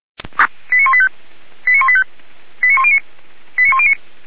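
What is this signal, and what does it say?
Five-tone selective call (ZVEI), heard as quick runs of five tones, tones of about 70 ms. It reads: 89185 89185 89109 89109
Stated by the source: ZVEI five-tone call received by SDR, Feuerwehr Orlamünde.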